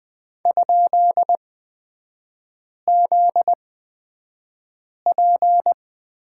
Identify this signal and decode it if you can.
Morse code, heard as a keyed tone, about 20 wpm, spaced as for 5 wpm, 700 Hz. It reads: ?ZP
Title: 243 Lesson 39e - Letter z - Three Characters 20wpm